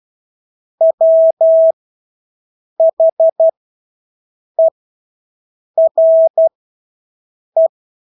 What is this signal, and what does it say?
Morse code, heard as a keyed tone, about 12 wpm, spaced as for 6 wpm, 650 Hz. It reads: WHERE
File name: Engelske ord 12-6 wpm